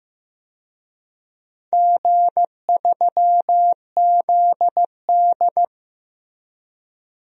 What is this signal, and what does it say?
Morse code, heard as a keyed tone, about 15 wpm, 700 Hz. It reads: G3ZD
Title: DXCC-100 Call Signs - 4 Characters - Repeated 3x - 15wpm